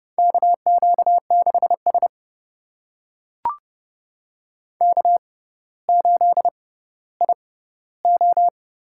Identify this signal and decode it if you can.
Morse code, heard as a keyed tone, about 30 wpm, spaced as for 15 wpm, 690 Hz. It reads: KQ6S K8IO
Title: USA Call Signs - 4 Characters - 6x Character Spacing - 30wpm